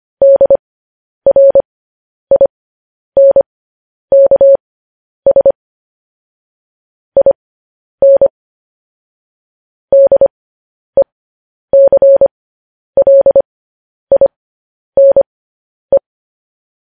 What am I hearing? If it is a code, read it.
Morse code, heard as a keyed tone, about 25 wpm, spaced as for 10 wpm, 550 Hz. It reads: DRINKS IN DECLINE
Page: News Headlines in Morse Code at 10 WPM